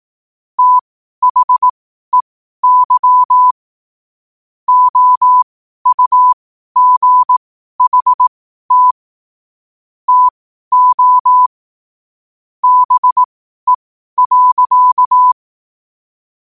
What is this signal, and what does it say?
Morse code, heard as a keyed tone, about 18 wpm, spaced as for 12 wpm, 1000 Hz. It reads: THEY OUGHT TO BE.
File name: qotdfiles / 12wpm